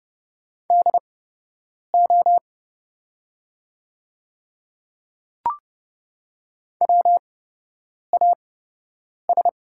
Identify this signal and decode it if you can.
Morse code, heard as a keyed tone, about 30 wpm, spaced as for 8 wpm, 700 Hz. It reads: DO WAS
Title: 8x Character Spacing - Top 500 Words - 30wpm